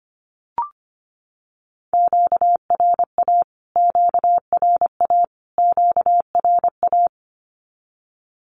Morse code, heard as a keyed tone, about 25 wpm, 700 Hz. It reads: QRA QRA QRA
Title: Q-Codes Repeated 3x 25wpm